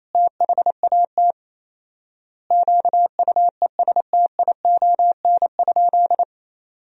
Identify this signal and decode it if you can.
Morse code, heard as a keyed tone, about 28 wpm, 700 Hz. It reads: THAT QUESTION?
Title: 4x Word-Spacing - Sentences from Top 400 Words - 28wpm